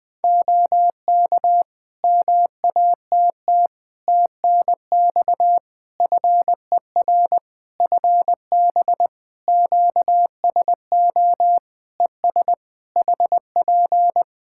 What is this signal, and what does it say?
Morse code, heard as a keyed tone, about 20 wpm, 700 Hz. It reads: OK MATT TNX FER FB QSO ES HP
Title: Practice QSO - Exchange 4 - Ending - 20wpm